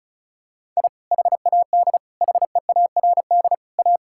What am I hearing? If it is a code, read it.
Morse code, heard as a keyed tone, about 36 wpm, 700 Hz. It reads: I HAD HEARD A
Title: Sentences from Top 1000 Words - Review Part 6 - 35wpm